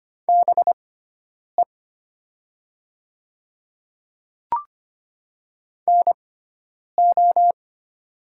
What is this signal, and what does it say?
Morse code, heard as a keyed tone, about 25 wpm, spaced as for 9 wpm, 700 Hz. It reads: BE NO
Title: 6x Character Spacing - Top 100 Words - 25wpm